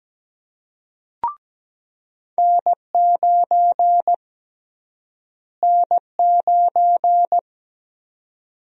Morse code, heard as a keyed tone, about 17 wpm, 700 Hz. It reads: N9 N9
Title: USA Call Sign Prefixes - Repeated 2x - 17wpm